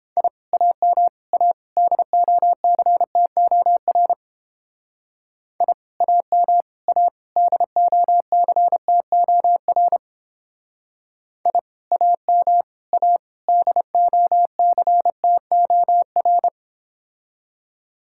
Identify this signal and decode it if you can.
Morse code, heard as a keyed tone, about 28 wpm, 700 Hz. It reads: I AM A DOCTOR I AM A DOCTOR I AM A DOCTOR